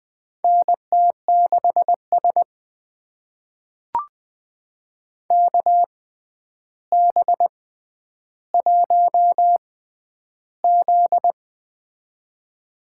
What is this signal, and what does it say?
Morse code, heard as a keyed tone, about 20 wpm, spaced as for 10 wpm, 700 Hz. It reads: NT6S KB1Z
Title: USA Call Signs - 4 Characters - 6x Character Spacing - 20wpm